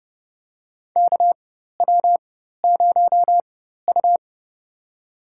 Morse code, heard as a keyed tone, about 32 wpm, spaced as for 14 wpm, 700 Hz. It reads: KW0U
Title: USA Call Signs - 4 Characters - 4x Character Spacing - 30wpm